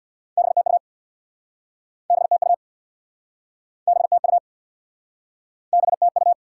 Morse code, heard as a keyed tone, about 66 wpm, 700 Hz. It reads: BTU BTU BTU BTU